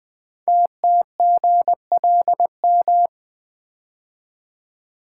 Morse code, heard as a keyed tone, about 20 wpm, 700 Hz. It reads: TTGLM
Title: Code Groups - 5 Letters - 20wpm